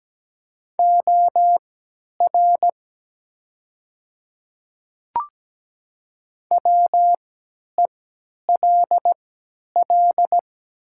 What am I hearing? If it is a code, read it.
Morse code, heard as a keyed tone, about 17 wpm, spaced as for 10 wpm, 700 Hz. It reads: OR WELL
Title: 3x Character Spacing - Top 200 Words - 17wpm